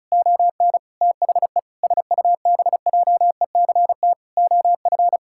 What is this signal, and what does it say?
Morse code, heard as a keyed tone, about 34 wpm, 700 Hz. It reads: ON THE SUBJECT OF
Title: Sentences from Top 1600 Words 35wpm